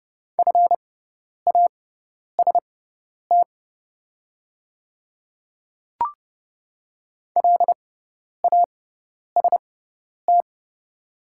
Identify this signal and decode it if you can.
Morse code, heard as a keyed tone, about 30 wpm, spaced as for 10 wpm, 700 Hz. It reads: FAST LAST
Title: 6x Character Spacing - Top 500 Words - 30wpm